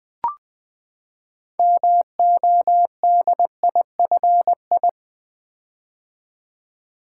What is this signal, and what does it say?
Morse code, heard as a keyed tone, about 20 wpm, 700 Hz. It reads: MODIFI
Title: Build A Word - 6 Letters At A Time - 12-Letter Words - 20wpm